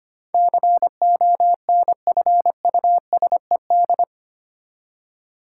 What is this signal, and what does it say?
Morse code, heard as a keyed tone, about 25 wpm, 700 Hz. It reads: CONFUSED
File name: Eight Letter Words - 25wpm